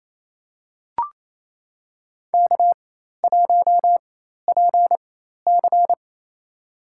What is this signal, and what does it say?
Morse code, heard as a keyed tone, about 28 wpm, spaced as for 13 wpm, 700 Hz. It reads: K1PC
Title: USA Call Signs - 4 Characters - 4x Character Spacing - 28wpm